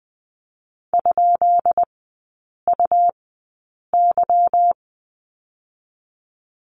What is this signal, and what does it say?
Morse code, heard as a keyed tone, about 20 wpm, spaced as for 8 wpm, 700 Hz. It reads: ?UY
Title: 166 Lesson 27e - Number 7 - Three Characters 20wpm